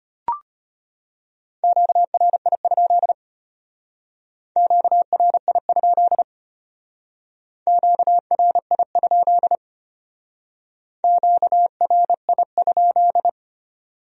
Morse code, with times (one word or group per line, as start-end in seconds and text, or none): 1.6-3.1 QRI?
4.6-6.2 QRI?
7.7-9.6 QRI?
11.0-13.3 QRI?